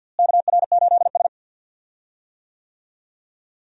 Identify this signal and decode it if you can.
Morse code, heard as a keyed tone, about 52 wpm, 700 Hz. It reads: KR8S
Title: USA Call Signs - 4 Characters - Repeated 3x - 50wpm